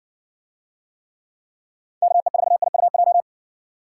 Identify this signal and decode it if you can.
Morse code, heard as a keyed tone, about 54 wpm, 700 Hz. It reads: KE4IUJ